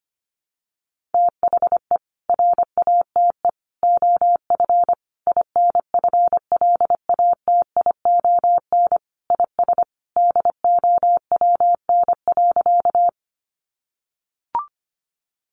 Morse code, heard as a keyed tone, about 25 wpm, 700 Hz. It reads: THE RATE OF INFLATION IS DOWN.
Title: Sentences from Top 1100 Words 25wpm